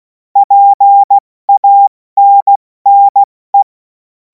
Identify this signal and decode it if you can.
Morse code, heard as a keyed tone, about 16 wpm, 800 Hz. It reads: PANNE
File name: Morsekurs / Witze / Witz 03. Februar 2026 70 F80